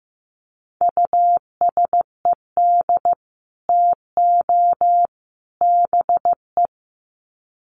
Morse code, heard as a keyed tone, about 15 wpm, 700 Hz. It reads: USED TO BE
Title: Sets of 3 Words 15wpm